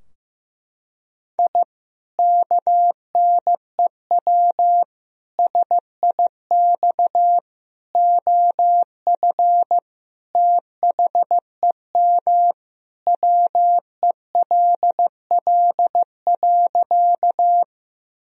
Morse code, heard as a keyed tone, about 15 wpm, 700 Hz. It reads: I KNEW SIX OF THEM WELL.
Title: Sentences from Top 1000 Words - Review Part 2 - 15wpm